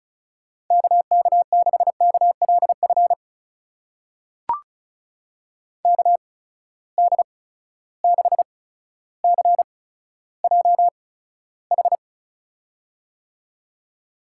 Morse code, heard as a keyed tone, about 36 wpm, spaced as for 15 wpm, 700 Hz. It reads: KK6KLF KD6CJH
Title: USA Call Signs - 8x Character Spacing - 35wpm